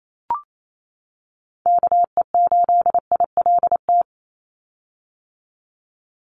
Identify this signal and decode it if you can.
Morse code, heard as a keyed tone, about 28 wpm, 700 Hz. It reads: KE8ILT
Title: USA Call Signs 28wpm